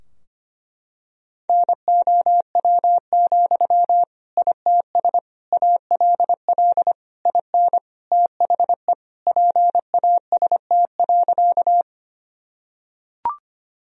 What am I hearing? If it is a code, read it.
Morse code, heard as a keyed tone, about 25 wpm, 700 Hz. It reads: NOW, ITS ALL IN THE PAST.
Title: Sentences from Top 1000 Words - Review Part 3 - 25wpm